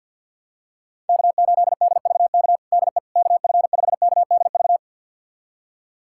Morse code, heard as a keyed tone, about 45 wpm, 700 Hz. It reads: K8DVK DE KF5KDV
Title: USA Callsign DE Callsign 50wpm